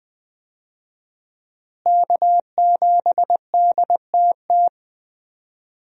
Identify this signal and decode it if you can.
Morse code, heard as a keyed tone, about 21 wpm, 700 Hz. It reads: K7DTT
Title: USA Call Signs - 5 Characters - Repeated 3x - 20wpm